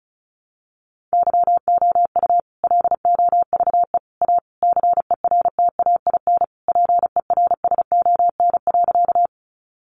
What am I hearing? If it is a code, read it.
Morse code, heard as a keyed tone, about 34 wpm, 700 Hz. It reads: YOU LOVE A CERTAIN PERSON.
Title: Sentences from Top 500 Words 35wpm